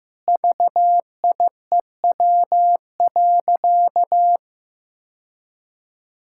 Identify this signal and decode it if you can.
Morse code, heard as a keyed tone, about 15 wpm, 700 Hz. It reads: VIEW.
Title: Sentences from Top 1000 Words - Review Part 6 - 15wpm